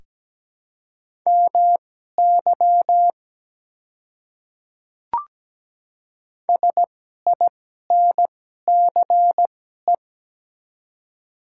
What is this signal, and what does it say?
Morse code, heard as a keyed tone, about 17 wpm, spaced as for 12 wpm, 690 Hz. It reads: MY SINCE
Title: 2x Character Spacing - Top 300 Words - 17wpm